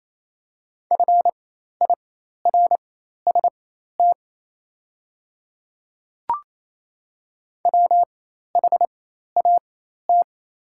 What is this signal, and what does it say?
Morse code, heard as a keyed tone, about 28 wpm, spaced as for 13 wpm, 700 Hz. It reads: FIRST WHAT
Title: 4x Character Spacing - Top 100 Words - 28wpm